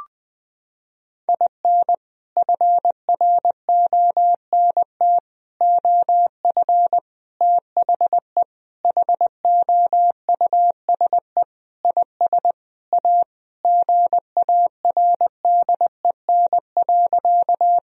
Morse code, heard as a keyed tone, about 20 wpm, 700 Hz. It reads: IN FRONT OF THE HOUSE IS A GARDEN.
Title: Sentences from Top 1000 Words - Review Part 4 - 20wpm